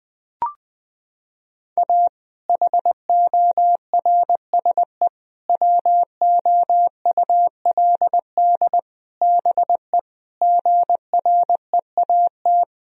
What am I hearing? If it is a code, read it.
Morse code, heard as a keyed tone, about 20 wpm, 700 Hz. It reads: A HORSE WOULD BE GREAT
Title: Sentences from Top 1000 Words - Review Part 1 - 20wpm